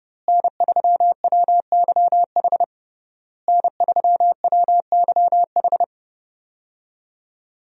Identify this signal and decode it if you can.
Morse code, heard as a keyed tone, about 28 wpm, 700 Hz. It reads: N3WYH N3WYH
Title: USA Call Signs - 5 Characters - Repeated 3x - 30wpm